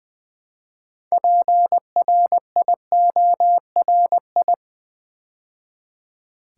Morse code, heard as a keyed tone, about 20 wpm, 700 Hz. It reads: PRIORI
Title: Build A Word - 2 Letters At A Time - 8-Letter Words - 20wpm